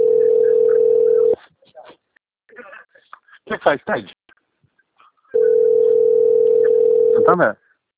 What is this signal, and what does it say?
Ringback tone (North America), 440 + 480 Hz, with speech.
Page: Jacarei - SP